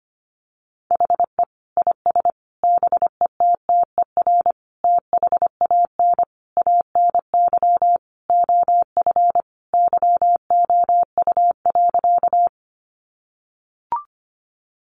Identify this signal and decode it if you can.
Morse code, heard as a keyed tone, about 25 wpm, 700 Hz. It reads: HE IS BETTER THAN ANY OF YOU.
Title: Sentences from Top 400 Words 25wpm